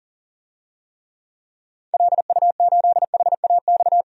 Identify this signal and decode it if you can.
Morse code, heard as a keyed tone, about 40 wpm, 700 Hz. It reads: LU8HAX
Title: DXCC-100 Call Signs - 6 Characters - 40wpm